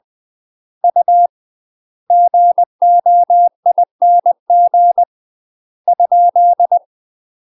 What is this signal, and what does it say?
Morse code, heard as a keyed tone, about 20 wpm, 700 Hz. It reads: U GOING ?
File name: UK MASTER QSOs TRAINING / 20WPM / UK MASTER QSOs 0001